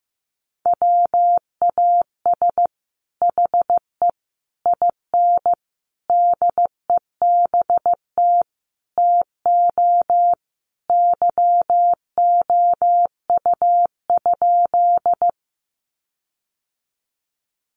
Morse code, heard as a keyed tone, about 15 wpm, 700 Hz. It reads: WAS HE IN DEBT TO YOU?
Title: Sentences from Top 1300 Words 15wpm